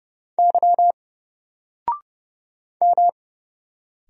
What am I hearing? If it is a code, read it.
Morse code, heard as a keyed tone, about 31 wpm, 700 Hz. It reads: Y M
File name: Single-Letter-Number-30wpm